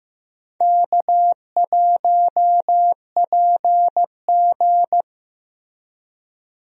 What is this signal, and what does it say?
Morse code, heard as a keyed tone, about 15 wpm, 700 Hz. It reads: K1PG
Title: USA Call Signs - 4 Characters - Repeated 3x - 15wpm